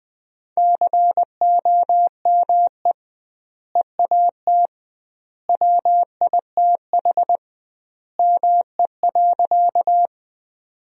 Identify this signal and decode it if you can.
Morse code, heard as a keyed tone, about 20 wpm, 700 Hz. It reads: COME EAT WITH ME.
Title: 2x Word-Spacing - Sentences from Top 300 Words - 20wpm